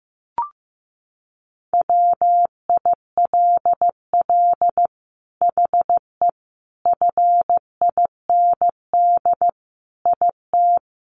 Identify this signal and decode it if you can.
Morse code, heard as a keyed tone, about 15 wpm, 700 Hz. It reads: WILL HE FIND IT
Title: Sentences from Top 600 Words 15wpm